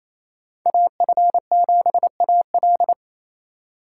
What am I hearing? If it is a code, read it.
Morse code, heard as a keyed tone, about 33 wpm, 700 Hz. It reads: AF7AL
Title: Speed Racing - USA 2x2 Call Signs 28wpm